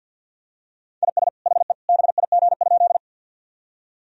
Morse code, heard as a keyed tone, about 50 wpm, 700 Hz. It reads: IS HE BIG?
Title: Sentences from Top 1000 Words - Review Part 1 - 50wpm